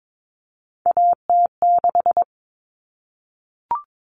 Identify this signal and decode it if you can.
Morse code, heard as a keyed tone, about 22 wpm, 700 Hz. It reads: AT6
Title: Common Cut Numbers - 3-Digit 22wpm